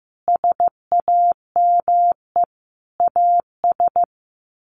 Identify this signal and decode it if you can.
Morse code, heard as a keyed tone, about 15 wpm, 700 Hz. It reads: SAME AS
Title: Sets of 5 Words Encore 15wpm